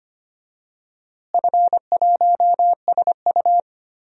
Morse code, heard as a keyed tone, about 25 wpm, 700 Hz. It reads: F1SU
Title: DXCC-100 Call Signs - 4 Characters - Repeated 2x - 25wpm